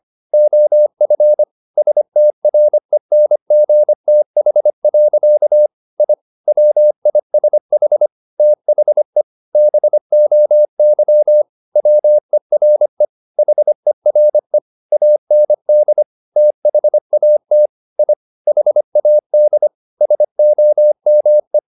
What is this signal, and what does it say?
Morse code, heard as a keyed tone, about 25 wpm, 600 Hz. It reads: OF STRENGTH. I WISH THE BOY WERE HERE AND THAT I HAD SOME
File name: the old man and the sea part 1 25wpm